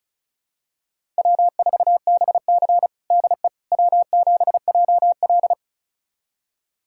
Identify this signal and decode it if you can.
Morse code, heard as a keyed tone, about 36 wpm, 700 Hz. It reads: W4BC DE W7JL
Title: USA Callsign DE Callsign - 4 Characters - 35wpm